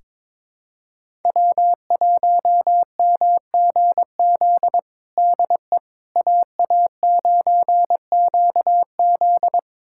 Morse code, heard as a keyed tone, about 21 wpm, 700 Hz. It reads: W1MGZ DE AA9QZ
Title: USA Callsign DE Callsign - 5 Characters - 22wpm